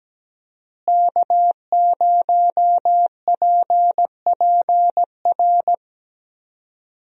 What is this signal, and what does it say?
Morse code, heard as a keyed tone, about 17 wpm, 700 Hz. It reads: K0PPR
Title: USA Call Signs - 5 Characters - Repeated 3x - 17wpm